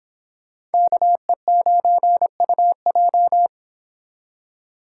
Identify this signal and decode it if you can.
Morse code, heard as a keyed tone, about 31 wpm, 700 Hz. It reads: KE9UJ